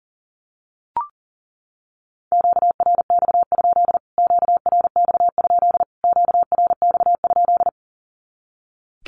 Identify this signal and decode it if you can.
Morse code, heard as a keyed tone, about 40 wpm, 700 Hz. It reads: QRX? QRX? QRX?
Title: Q-Codes Repeated 3x 40wpm